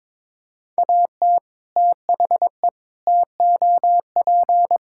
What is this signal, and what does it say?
Morse code, heard as a keyed tone, about 22 wpm, 700 Hz. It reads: AT THE TOP
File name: practice / 22wpm / Sentences with 3-Word Phrases - 22wpm